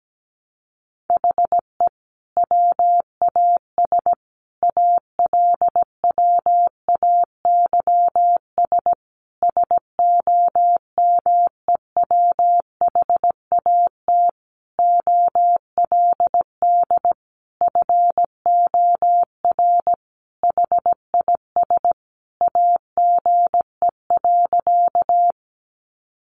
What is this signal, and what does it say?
Morse code, heard as a keyed tone, about 17 wpm, 700 Hz. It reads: HE WAS ALWAYS SOMEWHAT OLD FOR HIS AGE.
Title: Sentences from Top 1300 Words 17wpm